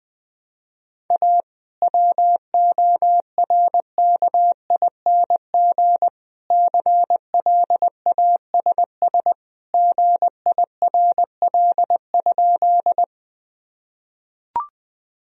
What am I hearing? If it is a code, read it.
Morse code, heard as a keyed tone, about 21 wpm, 700 Hz. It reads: A WORKING CLASS GIRL?
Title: Sentences from Top 1000 Words - Review Part 4 - 20wpm